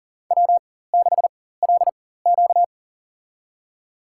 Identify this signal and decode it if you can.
Morse code, heard as a keyed tone, about 40 wpm, spaced as for 19 wpm, 700 Hz. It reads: W6LQ
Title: USA Call Signs - 4 Characters - 4x Character Spacing - 40wpm